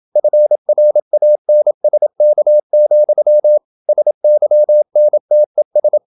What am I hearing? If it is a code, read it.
Morse code, heard as a keyed tone, about 27 wpm, 600 Hz. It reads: FRANSK, SYNTES
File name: Ferie QRQ 27wpm